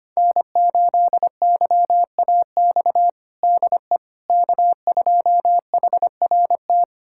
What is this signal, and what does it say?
Morse code, heard as a keyed tone, about 25 wpm, 700 Hz. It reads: N8YAX DE K2HRT
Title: USA Callsign DE Callsign - 5 Characters - 25wpm